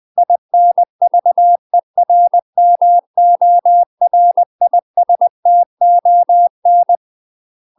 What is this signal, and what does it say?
Morse code, heard as a keyed tone, about 20 wpm, 700 Hz. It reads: INVERMORISTON
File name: UK MASTER QSOs TRAINING / 20WPM / UK MASTER QSOs 0066